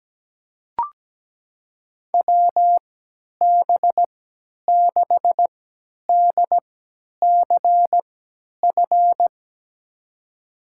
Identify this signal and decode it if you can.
Morse code, heard as a keyed tone, about 17 wpm, spaced as for 10 wpm, 700 Hz. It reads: WB6DCF
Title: USA Call Signs - 6 Characters - 3x Character Spacing - 17wpm